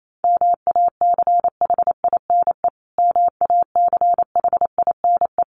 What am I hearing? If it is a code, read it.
Morse code, heard as a keyed tone, about 28 wpm, 700 Hz. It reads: MACHINE MACHINE
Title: Top 500 Words Repeated 28wpm